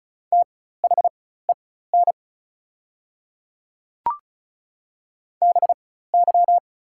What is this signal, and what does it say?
Morse code, heard as a keyed tone, about 36 wpm, spaced as for 16 wpm, 700 Hz. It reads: THEN BY
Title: 4x Character Spacing - Top 200 Words - 35wpm